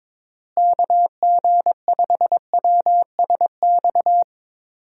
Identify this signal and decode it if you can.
Morse code, heard as a keyed tone, about 22 wpm, 700 Hz. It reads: KG5WSX
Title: USA Call Signs - 6 Characters - Repeated 3x - 22wpm